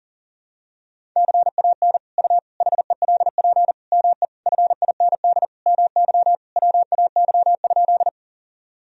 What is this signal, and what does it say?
Morse code, heard as a keyed tone, about 40 wpm, 700 Hz. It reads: CAN U HELP ME FIND MY WAY?
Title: Sentences from Top 200 Words and Abbreviations 40wpm